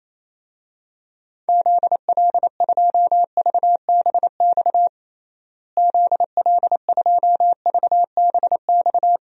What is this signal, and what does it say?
Morse code, heard as a keyed tone, about 28 wpm, 700 Hz. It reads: ZL2VBX ZL2VBX
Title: DXCC-100 Call Signs - 6 Characters - Repeated 2x - 28wpm